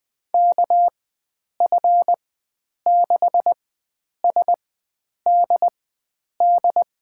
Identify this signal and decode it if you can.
Morse code, heard as a keyed tone, about 20 wpm, spaced as for 9 wpm, 700 Hz. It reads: KF6SDD
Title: USA Call Signs - 6 Characters - 4x Character Spacing - 20wpm